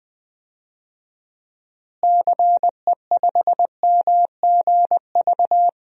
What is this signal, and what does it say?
Morse code, heard as a keyed tone, about 20 wpm, 700 Hz. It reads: CE5MGV
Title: DXCC-100 Call Signs - 6 Characters - Repeated 3x - 20wpm